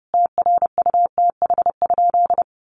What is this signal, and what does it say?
Morse code, heard as a keyed tone, about 30 wpm, 700 Hz. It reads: TRUTH?
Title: Sentences from Top 800 Words 30wpm